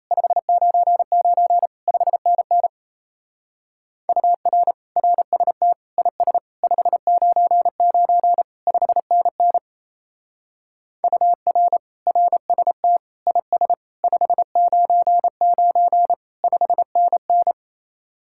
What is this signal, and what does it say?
Morse code, heard as a keyed tone, about 35 wpm, 700 Hz. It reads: 599 5NN UR RST IS 599 5NN UR RST IS 599 5NN
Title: Speed-Racing Common QSO Phrases Encore - 28wpm